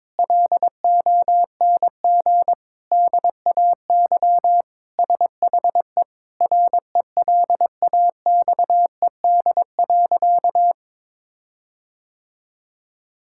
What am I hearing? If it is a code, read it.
Morse code, heard as a keyed tone, about 22 wpm, 700 Hz. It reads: LONG DAY SHE RELAXED.
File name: practice / 22wpm / Sentences with 4-Word Phrases - 22wpm